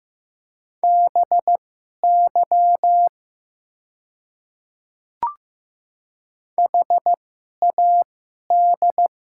Morse code, heard as a keyed tone, about 15 wpm, spaced as for 11 wpm, 700 Hz. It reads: BY HAD
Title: 2x Character Spacing - Top 200 Words - 15wpm